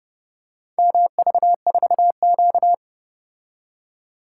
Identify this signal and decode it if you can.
Morse code, heard as a keyed tone, about 30 wpm, 700 Hz. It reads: MV4Q